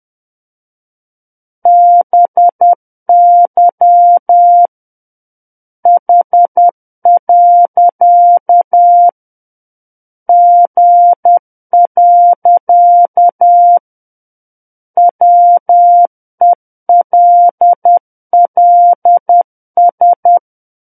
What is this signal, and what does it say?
Morse code, heard as a keyed tone, about 10 wpm, 700 Hz. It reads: BY H. G. WELLS
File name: Book One / 01-Chapter 1 - 10 WPM